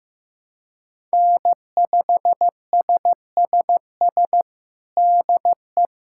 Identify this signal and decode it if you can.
Morse code, heard as a keyed tone, about 15 wpm, 700 Hz. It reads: N5SSS DE
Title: USA Callsign DE Callsign - 5 Characters - 15wpm